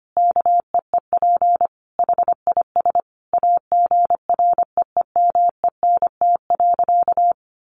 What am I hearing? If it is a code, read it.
Morse code, heard as a keyed tone, about 25 wpm, 700 Hz. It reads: KEEP HIS AGREEMENT.
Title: Sentences from Top 1500 Words 25wpm